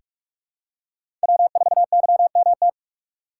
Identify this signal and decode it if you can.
Morse code, heard as a keyed tone, about 45 wpm, 700 Hz. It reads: W4YMT
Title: USA Call Signs - 5 Characters - 6x Character Spacing - 45wpm